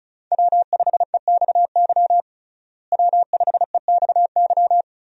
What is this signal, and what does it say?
Morse code, heard as a keyed tone, about 36 wpm, 700 Hz. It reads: W5EXY W5EXY
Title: USA Call Signs - 5 Characters - Repeated 2x - 35wpm